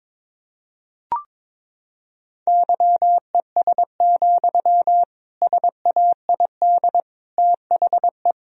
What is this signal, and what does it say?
Morse code, heard as a keyed tone, about 22 wpm, 700 Hz. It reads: YES, SAID THE
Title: Sentences from Top 1000 Words - Review Part 4 - 22wpm